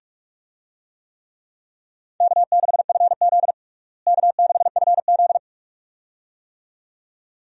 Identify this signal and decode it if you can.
Morse code, heard as a keyed tone, about 41 wpm, 700 Hz. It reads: K6FZ K6FZ
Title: USA Call Signs - 4 Characters - Repeated 2x - 45wpm